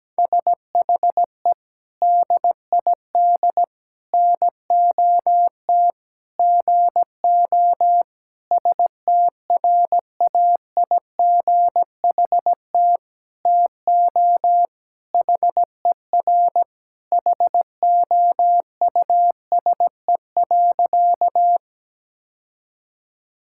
Morse code, heard as a keyed tone, about 17 wpm, 700 Hz. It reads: SHE DID NOT GO STRAIGHT TO HER HOUSE.
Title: Sentences from Top 1000 Words - Review Part 6 - 17wpm